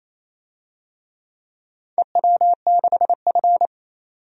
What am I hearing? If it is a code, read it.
Morse code, heard as a keyed tone, about 28 wpm, 700 Hz. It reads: EW6F
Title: DXCC-100 Call Signs - 4 Characters - 28wpm